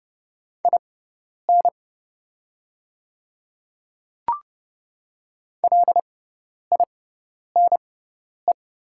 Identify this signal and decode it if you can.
Morse code, heard as a keyed tone, about 30 wpm, spaced as for 10 wpm, 700 Hz. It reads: IN LINE
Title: 6x Character Spacing - Top 200 Words - 30wpm